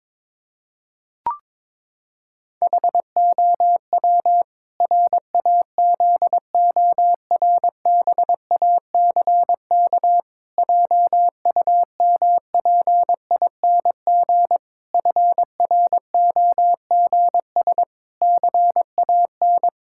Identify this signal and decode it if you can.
Morse code, heard as a keyed tone, about 22 wpm, 700 Hz. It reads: HOW RAZORBACK JUMPING FROGS CAN
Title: Pangram Challenge v2 - 22wpm